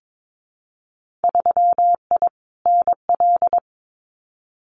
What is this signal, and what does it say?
Morse code, heard as a keyed tone, about 22 wpm, 700 Hz. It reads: 3I NL
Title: Winter Field Day Exchanges 22wpm